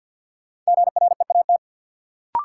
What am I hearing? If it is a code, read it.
Morse code, heard as a keyed tone, about 53 wpm, 700 Hz. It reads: GREAT
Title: Sets of 2 Words - New Word Drill - 50wpm